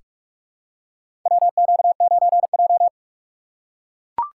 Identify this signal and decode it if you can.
Morse code, heard as a keyed tone, about 44 wpm, 700 Hz. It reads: WQ9J
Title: USA Call Signs - 4 Characters - 4x Character Spacing - 45wpm